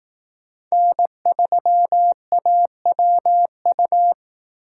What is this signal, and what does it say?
Morse code, heard as a keyed tone, about 18 wpm, 700 Hz. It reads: N3AWU
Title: Speed Racing - USA 1x3 Call Signs 15wpm